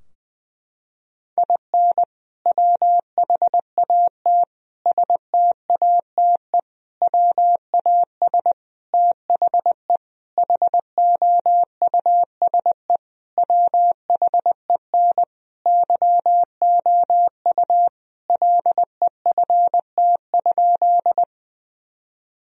Morse code, heard as a keyed tone, about 21 wpm, 700 Hz. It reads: IN WHAT STATE WAS THE HOUSE WHEN YOU LEFT?
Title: Sentences from Top 1000 Words - Review Part 2 - 20wpm